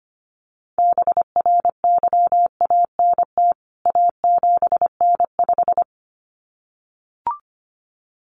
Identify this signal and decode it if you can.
Morse code, heard as a keyed tone, about 25 wpm, 700 Hz. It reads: BRYANT A7N5
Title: CWT Exchange 25wpm